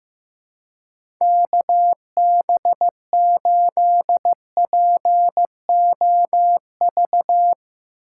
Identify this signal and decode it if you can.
Morse code, heard as a keyed tone, about 16 wpm, 700 Hz. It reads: KB8POV
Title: USA Call Signs - 6 Characters - 4x Character Spacing - 15wpm